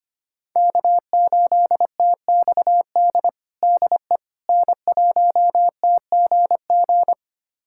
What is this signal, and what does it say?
Morse code, heard as a keyed tone, about 25 wpm, 700 Hz. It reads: K8TXD DE N1TGG
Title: USA Callsign DE Callsign - 5 Characters - 25wpm